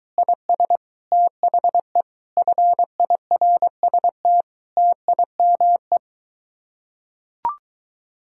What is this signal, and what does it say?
Morse code, heard as a keyed tone, about 23 wpm, 700 Hz. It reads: IS THE FIRST TIME